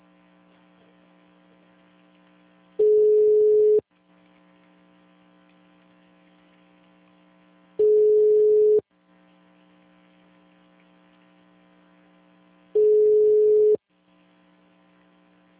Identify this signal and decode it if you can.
Ringback tone (Europe, 425 Hz).